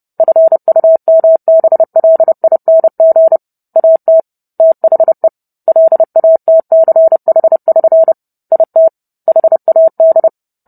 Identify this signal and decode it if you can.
Morse code, heard as a keyed tone, about 30 wpm, 650 Hz. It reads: FUMBLING AT THE LATCH<SN> IT HAD